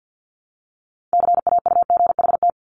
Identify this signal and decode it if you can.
Morse code, heard as a keyed tone, about 50 wpm, 700 Hz. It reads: CAUGHT
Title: Multi-letter Phonemes 50wpm